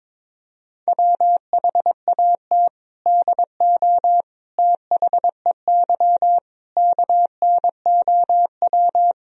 Morse code, heard as a keyed tone, about 22 wpm, 700 Hz. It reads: WHAT DO THEY KNOW
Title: Sentences from Top 1000 Words - Review Part 3 - 22wpm